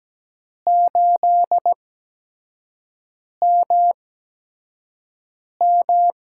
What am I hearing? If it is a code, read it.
Morse code, heard as a keyed tone, about 17 wpm, spaced as for 5 wpm, 700 Hz. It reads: 8MM
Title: USA Call Signs - 6 Characters - 8x Character Spacing - 17wpm